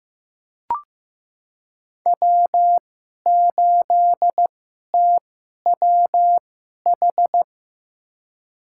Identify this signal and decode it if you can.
Morse code, heard as a keyed tone, about 15 wpm, spaced as for 11 wpm, 700 Hz. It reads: W8TWH
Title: USA Call Signs - 5 Characters - 2x Character Spacing - 15wpm